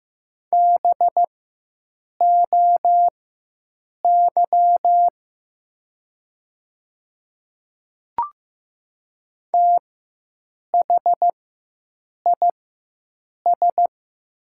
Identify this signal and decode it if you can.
Morse code, heard as a keyed tone, about 15 wpm, spaced as for 7 wpm, 700 Hz. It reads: BOY THIS
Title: 4x Character Spacing - Top 200 Words - 15wpm